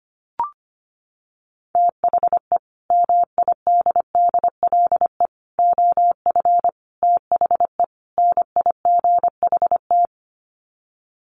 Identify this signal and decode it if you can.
Morse code, heard as a keyed tone, about 25 wpm, 700 Hz. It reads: THE MIDDLE OF THE NIGHT
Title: Sets of 5 Words 25wpm